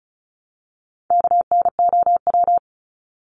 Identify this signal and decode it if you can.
Morse code, heard as a keyed tone, about 34 wpm, 700 Hz. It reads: KNOW
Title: Top 300 Words Repeated 35wpm